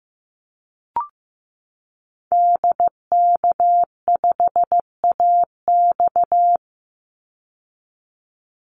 Morse code, heard as a keyed tone, about 15 wpm, 700 Hz. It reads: DK5AX
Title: Call Signs Repeated 2x - 15wpm